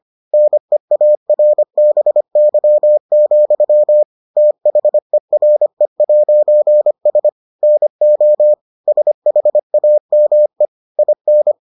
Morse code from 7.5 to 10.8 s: NO SHAME

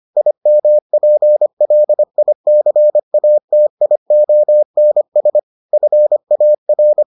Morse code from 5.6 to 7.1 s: FAR

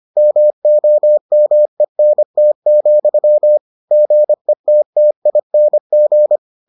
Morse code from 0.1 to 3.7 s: MOMENT,

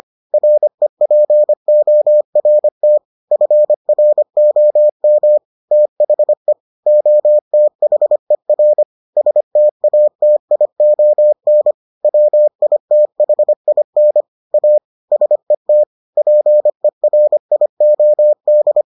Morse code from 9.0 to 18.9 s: STATION WITHIN A SET PERIOD

